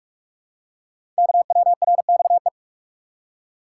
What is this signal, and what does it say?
Morse code, heard as a keyed tone, about 44 wpm, 690 Hz. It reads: KWRXE
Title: Code Groups - 5 Letters - 45wpm